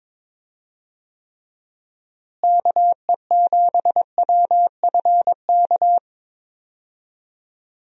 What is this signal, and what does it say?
Morse code, heard as a keyed tone, about 22 wpm, 700 Hz. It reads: KE7WFK
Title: USA Call Signs - 6 Characters - Repeated 2x - 22wpm